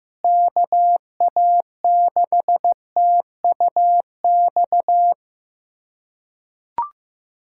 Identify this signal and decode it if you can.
Morse code, heard as a keyed tone, about 15 wpm, 700 Hz. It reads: KA6TUX
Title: USA Callsign DE Callsign 15wpm